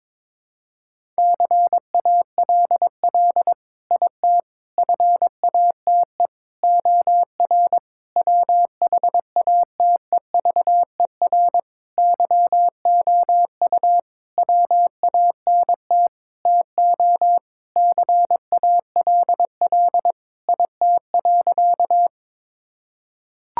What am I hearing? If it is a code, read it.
Morse code, heard as a keyed tone, about 22 wpm, 700 Hz. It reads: CALL IT FATE OR WHATEVER YOU WANT TO CALL IT.
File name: Sentences with 7-Word Phrases - 22wpm